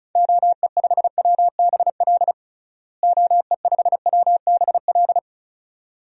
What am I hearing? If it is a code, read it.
Morse code, heard as a keyed tone, about 34 wpm, 700 Hz. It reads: OE5WBL OE5WBL
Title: DXCC-100 Call Signs - 6 Characters - Repeated 2x - 35wpm